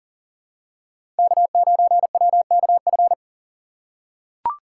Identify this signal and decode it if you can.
Morse code, heard as a keyed tone, about 37 wpm, 700 Hz. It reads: K9WKF
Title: USA Call Signs - 5 Characters - 4x Character Spacing - 40wpm